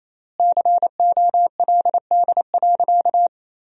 Morse code, heard as a keyed tone, about 28 wpm, 700 Hz. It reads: COLD.
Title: 4x Word-Spacing - Sentences from Top 400 Words - 28wpm